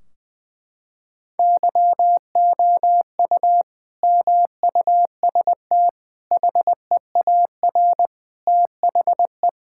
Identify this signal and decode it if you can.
Morse code, heard as a keyed tone, about 20 wpm, 700 Hz. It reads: YOU MUST HEAR THE
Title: Sentences from Top 1600 Words 20wpm